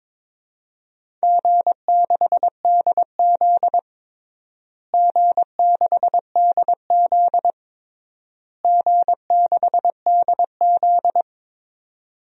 Morse code, heard as a keyed tone, about 22 wpm, 700 Hz. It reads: G6DZ G6DZ G6DZ
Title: DXCC-100 Call Signs - 4 Characters - Repeated 3x - 22wpm